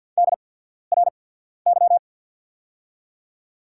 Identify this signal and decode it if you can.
Morse code, heard as a keyed tone, about 48 wpm, spaced as for 13 wpm, 700 Hz. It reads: DRY
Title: 8x Character Spacing - Top 500 Words - 50wpm